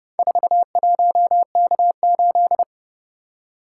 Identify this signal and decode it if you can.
Morse code, heard as a keyed tone, about 29 wpm, 700 Hz. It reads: 41K8